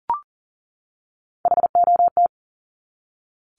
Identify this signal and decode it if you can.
Morse code, heard as a keyed tone, about 40 wpm, 700 Hz. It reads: HOT
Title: Top 200 Words 40wpm